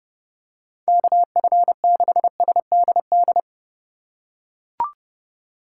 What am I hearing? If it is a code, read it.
Morse code, heard as a keyed tone, about 32 wpm, 700 Hz. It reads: KF6SDD
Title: USA Call Signs - 2x Character Spacing - 30wpm